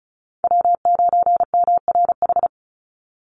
Morse code, heard as a keyed tone, about 34 wpm, 700 Hz. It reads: W9MRH
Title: NA SOTA Chaser Callsigns 35wpm